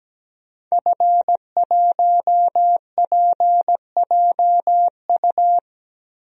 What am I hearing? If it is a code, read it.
Morse code, heard as a keyed tone, about 17 wpm, 700 Hz. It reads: F1PJU
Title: DXCC-100 Call Signs - 5 Characters - Repeated 2x - 17wpm